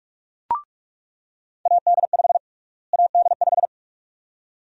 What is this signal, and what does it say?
Morse code, heard as a keyed tone, about 46 wpm, 700 Hz. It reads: AD5 AD5
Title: USA Call Sign Prefixes - Repeated 2x - 45wpm